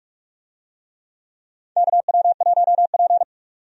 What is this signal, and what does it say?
Morse code, heard as a keyed tone, about 46 wpm, 700 Hz. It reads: KW1P
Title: USA Call Signs - 4 Characters - 45wpm